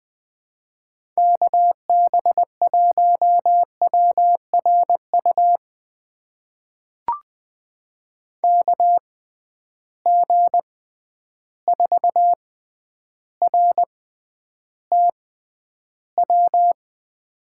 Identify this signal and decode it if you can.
Morse code, heard as a keyed tone, about 21 wpm, spaced as for 10 wpm, 700 Hz. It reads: KB1WRU KG4RTW
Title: USA Call Signs - 6 Characters - 6x Character Spacing - 20wpm